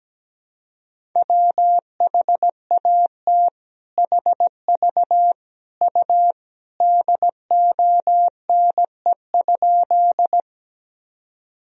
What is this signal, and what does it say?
Morse code, heard as a keyed tone, about 17 wpm, 700 Hz. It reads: WHAT HV U DONE?
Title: Sentences from Top 500 Words and Abbreviations 17wpm